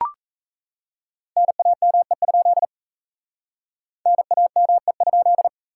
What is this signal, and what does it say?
Morse code, heard as a keyed tone, about 38 wpm, 700 Hz. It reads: NAME? NAME?